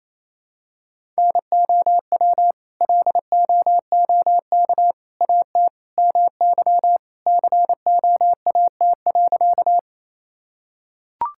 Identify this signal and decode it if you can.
Morse code, heard as a keyed tone, about 28 wpm, 700 Hz. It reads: NOW LOOK AT MY COAT.
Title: Sentences from Top 1600 Words 28wpm